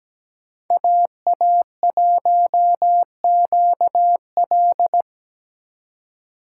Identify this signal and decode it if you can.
Morse code, heard as a keyed tone, about 20 wpm, 700 Hz. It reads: AA1QL
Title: Speed Racing - USA 2x2 Call Signs 17wpm